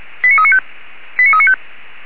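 Five-tone selective call (ZVEI), heard as quick runs of five tones, tones of about 70 ms. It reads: 89395 89395
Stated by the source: ZVEI five-tone call 89406 received by SDR.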